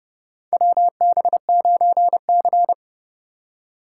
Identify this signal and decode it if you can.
Morse code, heard as a keyed tone, about 31 wpm, 700 Hz. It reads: WB9C
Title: USA Call Signs - 4 Characters - 6x Character Spacing - 30wpm